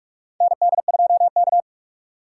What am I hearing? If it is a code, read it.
Morse code, heard as a keyed tone, about 43 wpm, 700 Hz. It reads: ND2K
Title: USA Call Signs - 4 Characters - 2x Character Spacing - 45wpm